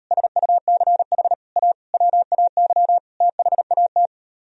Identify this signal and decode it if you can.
Morse code, heard as a keyed tone, about 38 wpm, 700 Hz. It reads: SUCH A WAY THAT